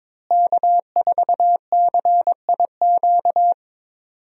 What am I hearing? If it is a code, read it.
Morse code, heard as a keyed tone, about 22 wpm, 700 Hz. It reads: K4CIQ
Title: USA Call Signs - 5 Characters - Repeated 3x - 22wpm